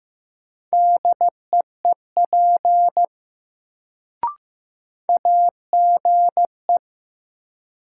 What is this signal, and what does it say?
Morse code, heard as a keyed tone, about 15 wpm, 700 Hz. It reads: DEEP AGE
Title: Sentences from Top 500 Words - New Word Drill - 15wpm